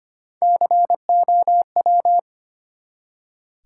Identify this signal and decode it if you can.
Morse code, heard as a keyed tone, about 25 wpm, 700 Hz. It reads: COW
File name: practice / 25wpm / Easy Words with Hints - 25wpm